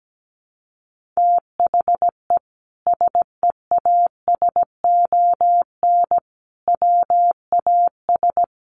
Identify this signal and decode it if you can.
Morse code, heard as a keyed tone, about 17 wpm, 700 Hz. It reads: THE SEASON WAS
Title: Sentences from Top 1400 Words 17wpm